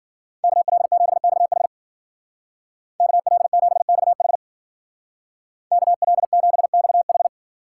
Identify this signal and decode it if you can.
Morse code, heard as a keyed tone, about 52 wpm, 700 Hz. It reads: KL7XH KL7XH KL7XH